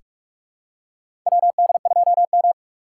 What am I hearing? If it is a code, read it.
Morse code, heard as a keyed tone, about 43 wpm, 700 Hz. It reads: WD2M
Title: USA Call Signs - 4 Characters - 3x Character Spacing - 45wpm